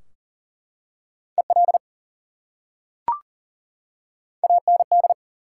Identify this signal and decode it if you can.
Morse code, heard as a keyed tone, about 40 wpm, 700 Hz. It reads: EL AND
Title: Top 180 QSO Elements 40wpm